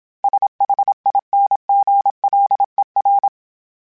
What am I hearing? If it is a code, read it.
Morse code, heard as a keyed tone, about 26 wpm, 800 Hz. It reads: SHINGLER
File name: Special INGs / Special INGs-mp3s / Hidden ings @25